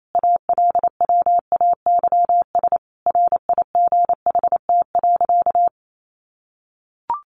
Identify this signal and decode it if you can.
Morse code, heard as a keyed tone, about 28 wpm, 700 Hz. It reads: ALWAYS RIGHT.
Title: Sentences from Top 1100 Words 28wpm